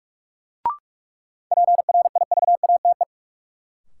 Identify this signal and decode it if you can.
Morse code, heard as a keyed tone, about 43 wpm, 700 Hz. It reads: PRIVATE
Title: Sentences from Top 1000 Words - New Word Drill - 45wpm